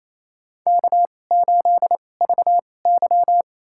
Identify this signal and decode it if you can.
Morse code, heard as a keyed tone, about 28 wpm, spaced as for 20 wpm, 700 Hz. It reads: K8VY
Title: USA Call Signs - 4 Characters - 2x Character Spacing - 28wpm